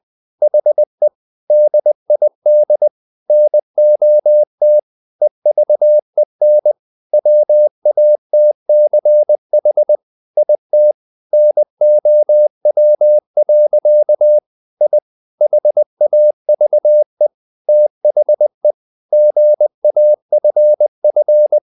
Morse code, heard as a keyed tone, about 20 wpm, 600 Hz. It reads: HE DID NOT EVEN WATCH IT NOW. I HAVE THE GAFF
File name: the old man and the sea part 2 20wpm